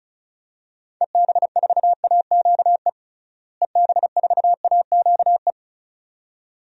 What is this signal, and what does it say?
Morse code, heard as a keyed tone, about 34 wpm, 700 Hz. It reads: EB4AQE EB4AQE
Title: DXCC-100 Call Signs - 6 Characters - Repeated 2x - 35wpm